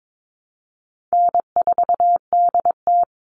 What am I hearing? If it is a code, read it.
Morse code, heard as a keyed tone, about 22 wpm, 700 Hz. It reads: N4DT
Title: Easy Call Signs - 22wpm